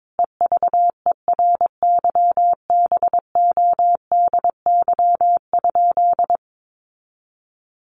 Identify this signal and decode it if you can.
Morse code, heard as a keyed tone, about 22 wpm, 700 Hz. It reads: EVERYBODY?
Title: Sentences from Top 1400 Words 22wpm